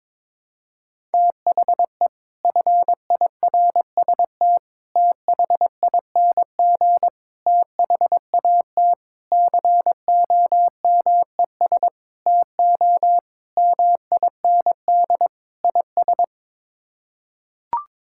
Morse code, heard as a keyed tone, about 22 wpm, 700 Hz. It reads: THE FIRST THING THAT COMES TO MIND IS
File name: practice / 22wpm / Sets of 8 Words - 22wpm